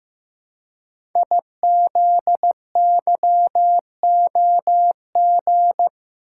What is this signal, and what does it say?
Morse code, heard as a keyed tone, about 15 wpm, 700 Hz. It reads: IZYOG
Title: Code Groups - 5 Letters - 15wpm